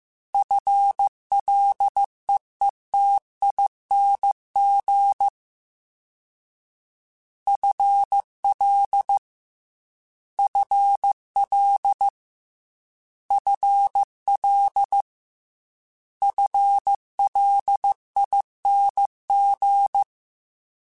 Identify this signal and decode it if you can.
Morse code, heard as a keyed tone, about 15 wpm, 800 Hz. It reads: FLEETING FL FL FL FLING
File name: FL / FL mp3s / FL Prefixes 3x2 @13